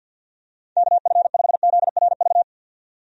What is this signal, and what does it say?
Morse code, heard as a keyed tone, about 52 wpm, 700 Hz. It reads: KF5ZRV
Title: USA Call Signs - 6 Characters - 50wpm